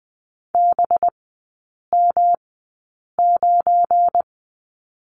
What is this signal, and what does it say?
Morse code, heard as a keyed tone, about 20 wpm, spaced as for 8 wpm, 690 Hz. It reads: BM9
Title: 191 Lesson 31e - Letter v - Three Characters 20wpm